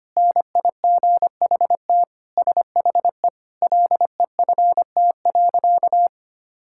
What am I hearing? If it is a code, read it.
Morse code, heard as a keyed tone, about 25 wpm, 700 Hz. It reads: NIGHT SHE LEFT.
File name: practice / 25wpm / Sentences with 4-Word Phrases - 25wpm